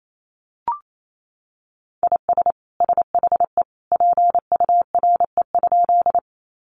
Morse code, heard as a keyed tone, about 28 wpm, 700 Hz. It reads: IS SHE PURE?
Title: Sentences from Top 1500 Words 28wpm